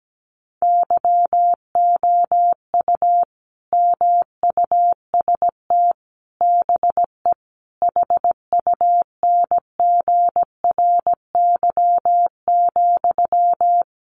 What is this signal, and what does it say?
Morse code, heard as a keyed tone, about 17 wpm, 700 Hz. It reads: YOU MUST BE HUNGRY,
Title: Sentences from Top 1400 Words 17wpm